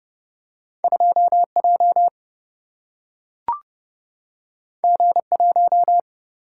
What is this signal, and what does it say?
Morse code, heard as a keyed tone, about 30 wpm, 700 Hz. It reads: 2 J G 1
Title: Single-Letter Single-Number 30wpm